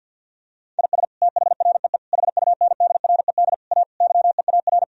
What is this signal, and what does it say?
Morse code, heard as a keyed tone, about 50 wpm, 700 Hz. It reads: IS THREE HUNDRED A YEAR
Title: Sentences from Top 1000 Words - Review Part 2 - 50wpm